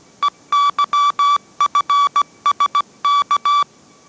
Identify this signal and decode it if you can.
Morse code, heard as a keyed tone, about 17 wpm, 1150 Hz. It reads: EYFSK